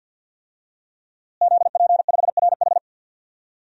Morse code, heard as a keyed tone, about 52 wpm, 700 Hz. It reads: ZP5RH
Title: DXCC-100 Call Signs - 5 Characters - Repeated 3x - 50wpm